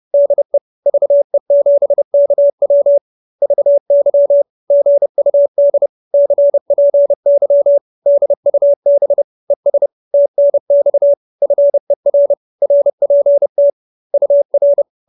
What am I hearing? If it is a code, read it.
Morse code, heard as a keyed tone, about 30 wpm, 570 Hz. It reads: DE VE7KW VY GUD CPY DUB ES TNX FER RPT UR
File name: QSO314 30